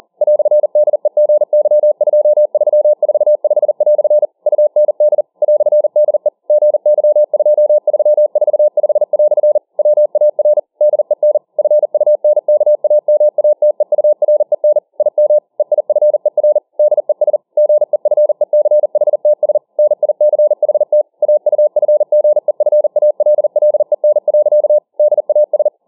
Morse code, heard as a keyed tone, about 40 wpm, 600 Hz. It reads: "DEGY2345" UND "DE GY2345" WAR DEN FUNKAMATEUREN IM EIFER DES GEFECHTS NICHT AUFGEFALLEN. DAS